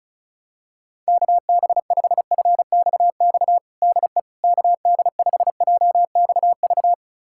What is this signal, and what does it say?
Morse code, heard as a keyed tone, about 36 wpm, 700 Hz. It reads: KB5FXX DE KD5JXV
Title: USA Callsign DE Callsign - 6 Characters - 35wpm